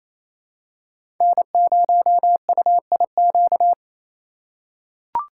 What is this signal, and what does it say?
Morse code, heard as a keyed tone, about 28 wpm, 700 Hz. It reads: N0UIQ
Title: USA Call Signs - 5 Characters - 4x Character Spacing - 28wpm